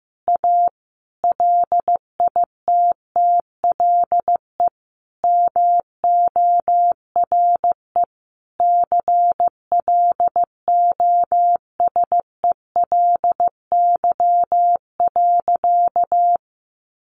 Morse code, heard as a keyed tone, about 15 wpm, 700 Hz. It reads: A LITTLE MORE CLOSELY.
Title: Sentences from Top 1400 Words 15wpm